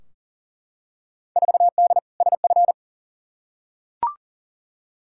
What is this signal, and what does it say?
Morse code, heard as a keyed tone, about 40 wpm, 700 Hz. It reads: 4D SF
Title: Field Day Exchanges Encore - 40wpm